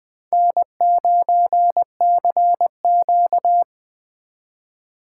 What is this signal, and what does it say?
Morse code, heard as a keyed tone, about 21 wpm, 700 Hz. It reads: N9CQ
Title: Speed Racing - USA 1x2 Call Signs 17wpm